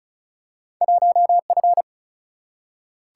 Morse code, heard as a keyed tone, about 36 wpm, 700 Hz. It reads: 1F
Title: Single-Letter Single-Number 35wpm